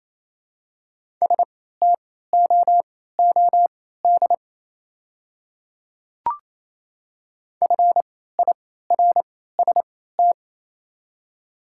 Morse code, heard as a keyed tone, about 28 wpm, spaced as for 16 wpm, 690 Hz. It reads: STOOD FIRST
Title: 3x Character Spacing - Top 500 Words - 28wpm